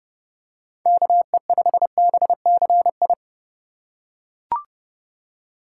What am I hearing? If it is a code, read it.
Morse code, heard as a keyed tone, about 29 wpm, 700 Hz. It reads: KE5BCI
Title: USA Call Signs - 6 Characters - 4x Character Spacing - 30wpm